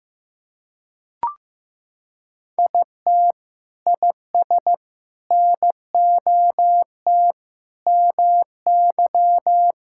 Morse code, heard as a keyed tone, about 15 wpm, 700 Hz. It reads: IT IS NOT MY
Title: Sentences from Top 1000 Words - Review Part 5 - 15wpm